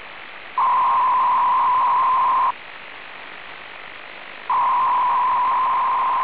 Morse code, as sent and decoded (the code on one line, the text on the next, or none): none